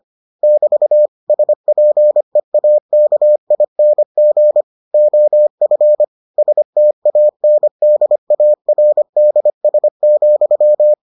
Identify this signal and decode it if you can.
Morse code, heard as a keyed tone, about 25 wpm, 600 Hz. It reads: <BT> SPEAKING OF STANDARDS,